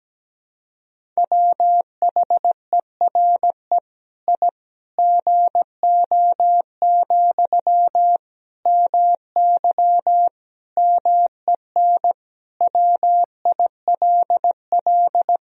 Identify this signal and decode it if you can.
Morse code, heard as a keyed tone, about 17 wpm, 700 Hz. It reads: WHERE I GO, MY MEN WILL
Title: Sentences from Top 1000 Words - Review Part 1 - 17wpm